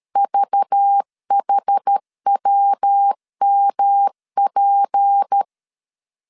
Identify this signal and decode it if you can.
Morse code, heard as a keyed tone, about 13 wpm, 800 Hz. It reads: VHWMP
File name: LEZIONI / 7 GF9Z / 7 GF9 12wpm 2319